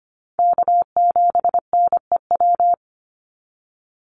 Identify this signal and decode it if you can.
Morse code, heard as a keyed tone, about 25 wpm, 700 Hz. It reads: K7NEW
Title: NA SOTA Chaser Callsigns 25wpm